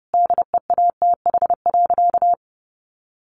Morse code, heard as a keyed tone, about 30 wpm, 700 Hz. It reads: DEATH.
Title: Sentences from Top 1100 Words 30wpm